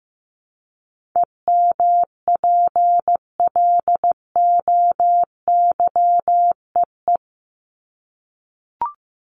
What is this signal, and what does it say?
Morse code, heard as a keyed tone, about 15 wpm, 700 Hz. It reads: EMPLOYEE
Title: Top 1501-1750 Words 15wpm